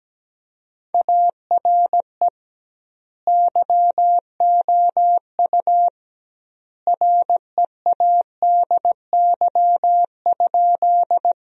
Morse code, heard as a keyed tone, about 17 wpm, 700 Hz. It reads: ARE YOU READY?
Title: 2x Word-Spacing - Sentences from Top 400 Words - 17wpm